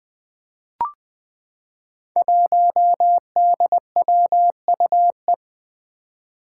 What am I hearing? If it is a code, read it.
Morse code, heard as a keyed tone, about 20 wpm, 700 Hz. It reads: 1DWUE
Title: Five Challenging Letter-Number Combinations 20wpm